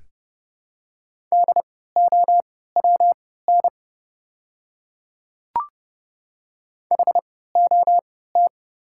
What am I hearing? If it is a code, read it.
Morse code, heard as a keyed tone, about 30 wpm, spaced as for 17 wpm, 700 Hz. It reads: DOWN HOT
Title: 3x Character Spacing - Top 100 Words - 30wpm